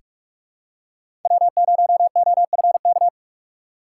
Morse code, heard as a keyed tone, about 46 wpm, 700 Hz. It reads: W0OFK
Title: USA Call Signs - 5 Characters - 3x Character Spacing - 45wpm